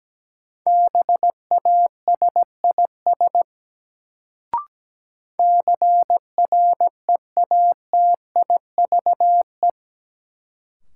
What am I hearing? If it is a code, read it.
Morse code, heard as a keyed tone, about 17 wpm, 700 Hz. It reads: BASIS CREATIVE
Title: Sentences from Top 1000 Words - Review Part 5 - New Word Drill - 17wpm